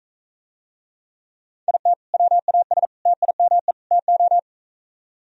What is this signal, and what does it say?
Morse code, heard as a keyed tone, about 44 wpm, 700 Hz. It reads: IT WAS TIME TO